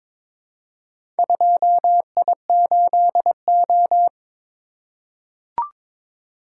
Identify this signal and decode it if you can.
Morse code, heard as a keyed tone, about 21 wpm, 700 Hz. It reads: 2I8O
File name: practice / 22wpm / Four Letter-Number Combinations - 22wpm